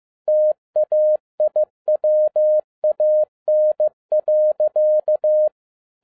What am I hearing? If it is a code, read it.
Morse code, heard as a keyed tone, about 15 wpm, 610 Hz. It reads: TAIWAN.
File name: 2011-07-mp3 / n07 A LOT HAS CHANGED 15wpm